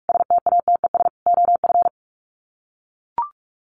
Morse code, heard as a keyed tone, about 44 wpm, 700 Hz. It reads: STATES OF
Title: Sets of 5 Words Encore2 45wpm